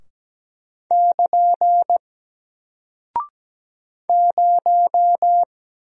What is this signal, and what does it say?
Morse code, heard as a keyed tone, about 17 wpm, 700 Hz. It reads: <KN> 0
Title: Single Character (Letter:Number:Punctuation:Prosign) Alt 17wpm